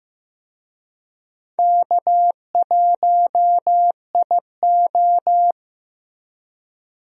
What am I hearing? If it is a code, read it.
Morse code, heard as a keyed tone, about 15 wpm, 700 Hz. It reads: K1IO
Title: USA Call Signs - 4 Characters - Repeated 2x - 15wpm